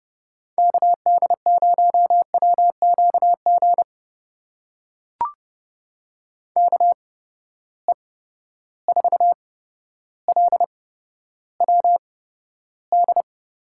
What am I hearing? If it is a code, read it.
Morse code, heard as a keyed tone, about 30 wpm, spaced as for 13 wpm, 700 Hz. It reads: KD0WQG KE4LWD
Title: USA Call Signs - 6 Characters - 8x Character Spacing - 30wpm